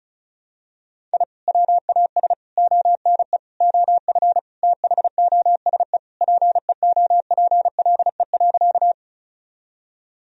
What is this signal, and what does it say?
Morse code, heard as a keyed tone, about 36 wpm, 700 Hz. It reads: I WAS ONE OF THOSE PEOPLE.
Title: Sentences from Top 1000 Words - Review Part 1 - 35wpm